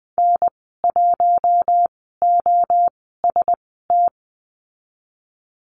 Morse code, heard as a keyed tone, about 20 wpm, spaced as for 15 wpm, 700 Hz. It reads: N1OST
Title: 048 Lesson 9g - Number 4 - Call Signs 20wpm